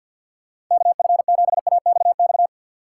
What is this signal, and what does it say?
Morse code, heard as a keyed tone, about 52 wpm, 700 Hz. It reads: KF7AXX
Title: USA Call Signs - 6 Characters - 4x Character Spacing - 50wpm